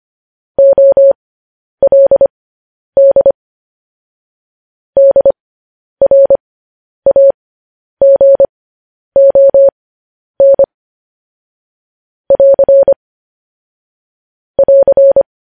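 Morse code, heard as a keyed tone, about 25 wpm, spaced as for 10 wpm, 550 Hz. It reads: OLD DRAGON <AR> <AR>
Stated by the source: Morse code news headlines transmission.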